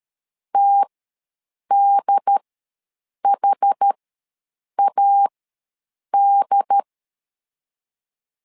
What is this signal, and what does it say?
Morse code, heard as a keyed tone, about 13 wpm, spaced as for 7 wpm, 800 Hz. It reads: TDHAD
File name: LEZIONI / 2 UHDN / 2 UHD ws 12wpm 1722